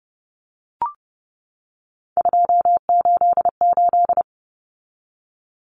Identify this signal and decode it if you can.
Morse code, heard as a keyed tone, about 30 wpm, 700 Hz. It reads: 288
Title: Common Cut Numbers - 3-Digit 30wpm